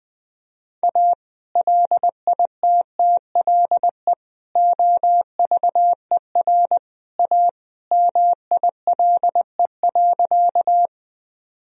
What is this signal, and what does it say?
Morse code, heard as a keyed tone, about 20 wpm, 700 Hz. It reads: A LITTLE OVER A MILE.
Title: Sentences from Top 1000 Words - Review Part 1 - 20wpm